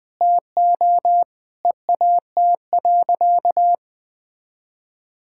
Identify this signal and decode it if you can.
Morse code, heard as a keyed tone, about 20 wpm, 700 Hz. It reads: TO EAT.
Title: Sentences from Top 1600 Words 20wpm